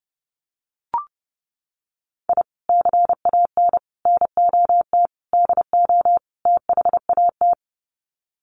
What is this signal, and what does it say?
Morse code, heard as a keyed tone, about 30 wpm, 700 Hz. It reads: I CAN NOT DO THAT
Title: Sets of 5 Words Encore2 30wpm